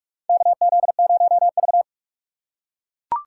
Morse code, heard as a keyed tone, about 43 wpm, 700 Hz. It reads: KZ0V
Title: USA Call Signs - 4 Characters - 3x Character Spacing - 45wpm